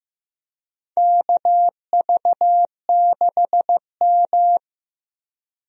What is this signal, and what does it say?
Morse code, heard as a keyed tone, about 15 wpm, 700 Hz. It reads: KV6M
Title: USA Call Signs - 4 Characters - Repeated 3x - 15wpm